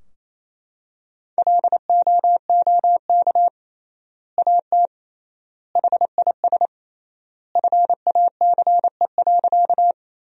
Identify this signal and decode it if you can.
Morse code, heard as a keyed tone, about 28 wpm, 700 Hz. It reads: LOOK AT HIS FACE.
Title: 3x Word-Spacing - Sentences from Top 300 Words - 28wpm